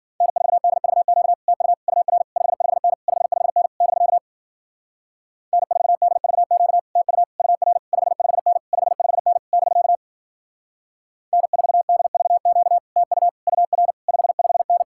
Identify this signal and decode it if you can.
Morse code, heard as a keyed tone, about 52 wpm, 700 Hz. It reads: N4DVQ TU UR 55N 55N <BK> N4DVQ TU UR 55N 55N <BK> N4DVQ TU UR 55N